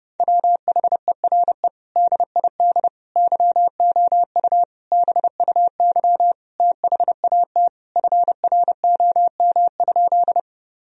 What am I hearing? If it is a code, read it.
Morse code, heard as a keyed tone, about 28 wpm, 700 Hz. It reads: WHERE DID YOU BUY THAT FROM?
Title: Sentences from Top 1000 Words - Review Part 4 - 30wpm